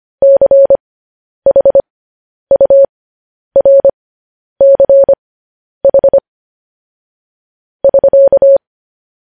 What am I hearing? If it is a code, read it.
Morse code, heard as a keyed tone, about 25 wpm, spaced as for 10 wpm, 550 Hz. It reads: CHURCH <SK>
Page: news-headlines-in-morse-code-at-10-wpm